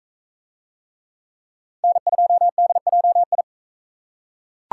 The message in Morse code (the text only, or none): N2DJI